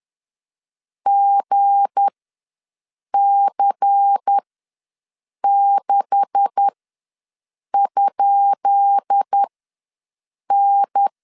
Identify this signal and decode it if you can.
Morse code, heard as a keyed tone, about 10 wpm, spaced as for 6 wpm, 800 Hz. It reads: GC6?N
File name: LEZIONI / 10 SIGNS / 10D COMA ws 10wpm 4314